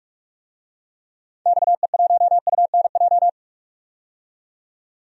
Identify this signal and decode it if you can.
Morse code, heard as a keyed tone, about 46 wpm, 700 Hz. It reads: XE1UNJ
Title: DXCC-100 Call Signs - 6 Characters - 45wpm